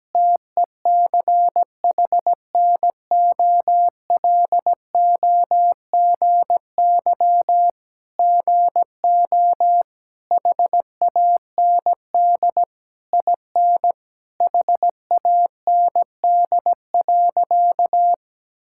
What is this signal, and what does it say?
Morse code, heard as a keyed tone, about 17 wpm, 700 Hz. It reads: TECHNOLOGY GO HAND IN HAND.
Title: Sentences from Top 1000 Words - Review Part 5 - 17wpm